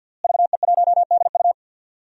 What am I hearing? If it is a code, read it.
Morse code, heard as a keyed tone, about 51 wpm, 700 Hz. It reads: VE1DU
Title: DXCC-100 Call Signs - 5 Characters - Repeated 2x - 50wpm